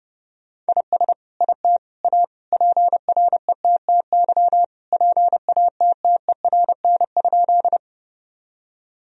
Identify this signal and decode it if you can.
Morse code, heard as a keyed tone, about 30 wpm, 700 Hz. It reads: IS IT A PRETTY PATTERN?
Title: Sentences from Top 1000 Words - Review Part 2 - 30wpm